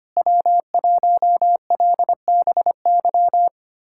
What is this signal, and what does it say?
Morse code, heard as a keyed tone, about 25 wpm, 700 Hz. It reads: W1LBY
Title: USA Callsign DE Callsign - 5 Characters - 25wpm